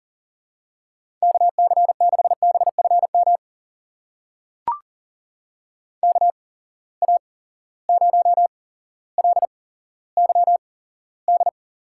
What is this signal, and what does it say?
Morse code, heard as a keyed tone, about 41 wpm, spaced as for 17 wpm, 700 Hz. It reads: KC6BFM KA0LYD
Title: USA Call Signs - 6 Characters - 8x Character Spacing - 40wpm